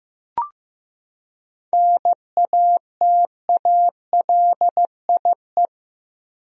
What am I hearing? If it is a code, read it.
Morse code, heard as a keyed tone, about 15 wpm, 700 Hz. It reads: NATALIE
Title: USA Common Names - 15wpm